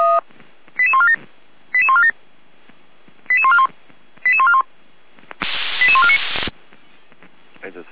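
Five-tone selective call (ZVEI), heard as quick runs of five tones, tones of about 70 ms. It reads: 88147 88147 88141 88141 88149
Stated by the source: ZVEI five-tone call received with an SDR.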